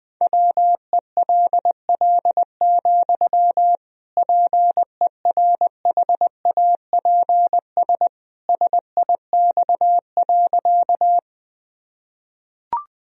Morse code, heard as a keyed tone, about 20 wpm, 700 Hz. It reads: WELL, PERHAPS SIX.
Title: Sentences from Top 1000 Words - Review Part 2 - 20wpm